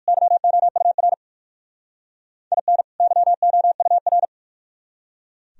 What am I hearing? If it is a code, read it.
Morse code, heard as a keyed tone, about 52 wpm, 700 Hz. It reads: YOUR IN YOUR
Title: Speed-Racing - Sets of 2 Words 45wpm